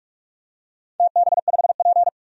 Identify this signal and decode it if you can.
Morse code, heard as a keyed tone, about 47 wpm, 700 Hz. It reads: TB5P
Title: DXCC-100 Call Signs - 4 Characters - 45wpm